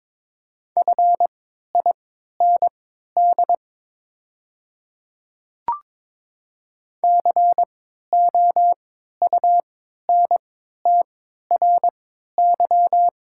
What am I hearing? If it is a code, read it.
Morse code, heard as a keyed tone, about 22 wpm, spaced as for 12 wpm, 690 Hz. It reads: FIND COUNTRY
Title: 3x Character Spacing - Top 300 Words - 22wpm